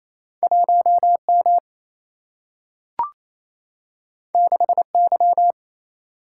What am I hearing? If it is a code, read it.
Morse code, heard as a keyed tone, about 28 wpm, 700 Hz. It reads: 1 M 6 Y
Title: Single-Letter Single-Number 28wpm